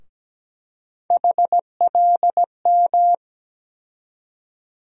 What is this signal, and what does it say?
Morse code, heard as a keyed tone, about 17 wpm, 700 Hz. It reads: HLM
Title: USA Call Sign Postfixes - 3 Letters 17wpm